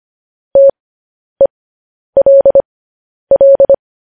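Morse code, heard as a keyed tone, about 25 wpm, spaced as for 10 wpm, 550 Hz. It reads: TELL
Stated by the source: Morse code transcription of news headlines sent at 10 wpm.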